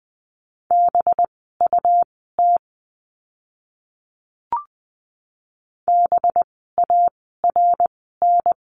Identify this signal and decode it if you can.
Morse code, heard as a keyed tone, about 20 wpm, spaced as for 14 wpm, 700 Hz. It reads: BUT BARN
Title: 185 Lesson 30f - Letter b - Words 20wpm